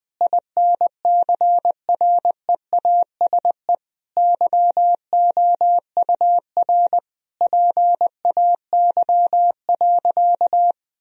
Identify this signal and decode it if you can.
Morse code, heard as a keyed tone, about 20 wpm, 700 Hz. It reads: INCREASE YOUR PAY.
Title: Sentences from Top 1000 Words - Review Part 4 - 20wpm